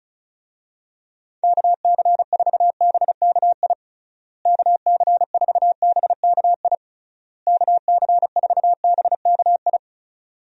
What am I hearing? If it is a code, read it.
Morse code, heard as a keyed tone, about 36 wpm, 700 Hz. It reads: KC4BKI KC4BKI KC4BKI
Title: USA Call Signs - 6 Characters - Repeated 3x - 35wpm